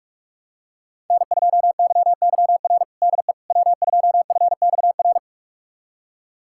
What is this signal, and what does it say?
Morse code, heard as a keyed tone, about 46 wpm, 700 Hz. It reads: N2YYR DE W2FXR
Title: USA Callsign DE Callsign - 5 Characters - 45wpm